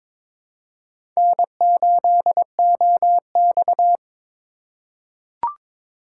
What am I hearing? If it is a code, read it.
Morse code, heard as a keyed tone, about 22 wpm, 700 Hz. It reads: N8OX
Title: USA Call Signs - 4 Characters - 2x Character Spacing - 22wpm